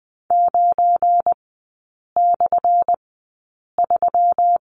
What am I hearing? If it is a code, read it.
Morse code, heard as a keyed tone, about 20 wpm, spaced as for 8 wpm, 700 Hz. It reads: 9/3
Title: 198 Lesson 32e - Letter k - Three Characters 20wpm